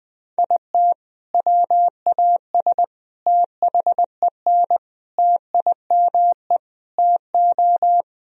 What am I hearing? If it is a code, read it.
Morse code, heard as a keyed tone, about 21 wpm, 700 Hz. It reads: IT WAS THEN TIME TO
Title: Sentences from Top 1000 Words - Review Part 1 - 20wpm